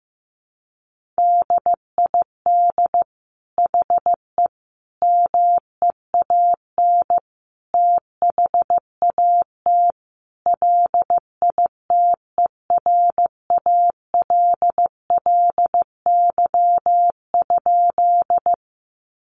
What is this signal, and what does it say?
Morse code, heard as a keyed tone, about 15 wpm, 700 Hz. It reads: DID HE MEAN THAT LITERALLY?
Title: Sentences from Top 1400 Words 15wpm